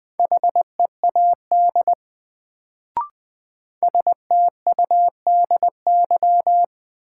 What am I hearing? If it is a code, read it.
Morse code, heard as a keyed tone, about 20 wpm, 700 Hz. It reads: HEAD STUDY
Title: Sentences from Top 300 Words - New Word Drill - 20wpm